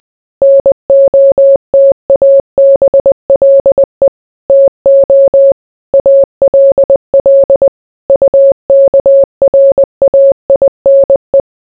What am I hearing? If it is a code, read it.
Morse code, heard as a keyed tone, about 20 wpm, 550 Hz. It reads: NOTABLE TO ALL UKRAINE